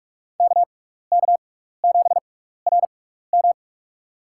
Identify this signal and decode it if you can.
Morse code, heard as a keyed tone, about 44 wpm, spaced as for 15 wpm, 690 Hz. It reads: KK7RM
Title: USA Call Signs - 5 Characters - 6x Character Spacing - 45wpm